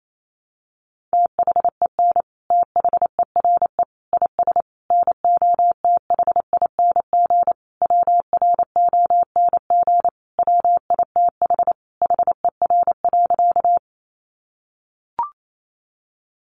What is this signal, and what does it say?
Morse code, heard as a keyed tone, about 28 wpm, 700 Hz. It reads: THEN THERE IS NOTHING WRONG WITH HER.
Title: Sentences from Top 900 Words 28wpm